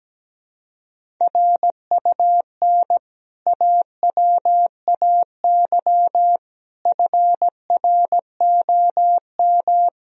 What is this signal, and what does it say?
Morse code, heard as a keyed tone, about 17 wpm, 700 Hz. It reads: RUN AWAY FROM
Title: Sentences from Top 1000 Words - Review Part 1 - 17wpm